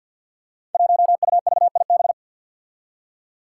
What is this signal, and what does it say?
Morse code, heard as a keyed tone, about 46 wpm, 700 Hz. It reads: 1UVIB
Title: Five Challenging Letter-Number Combinations 50wpm